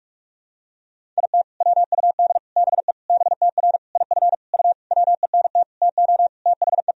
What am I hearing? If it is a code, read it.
Morse code, heard as a keyed tone, about 41 wpm, 700 Hz. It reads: IT WUD BE BTR IF U WENT TO THE
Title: Sentences from Top 400 Words and Abbreviations 45wpm